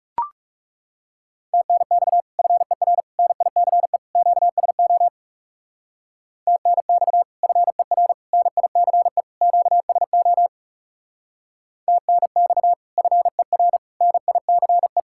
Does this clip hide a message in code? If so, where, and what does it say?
Yes: Morse code from 1.5 to 15.0 s, heard as a keyed tone, about 39 wpm, 700 Hz. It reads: TNX FER NICE QSO TNX FER NICE QSO TNX FER NICE